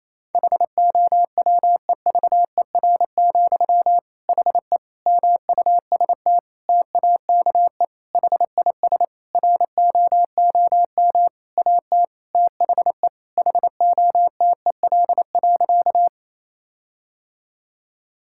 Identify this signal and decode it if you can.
Morse code, heard as a keyed tone, about 28 wpm, 700 Hz. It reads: HOWEVER, HE MUST TAKE HIS ROOM AT THE HOTEL.
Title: Sentences from Top 1600 Words 28wpm